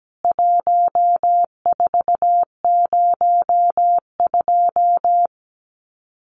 Morse code, heard as a keyed tone, about 17 wpm, 700 Hz. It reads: 1402
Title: CWT Exchange 17wpm